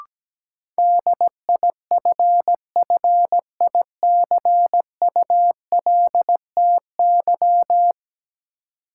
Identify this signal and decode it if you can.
Morse code, heard as a keyed tone, about 17 wpm, 700 Hz. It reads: DIFFICULTY
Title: Sentences from Top 1500 Words - New Word Drill Alt - 17wpm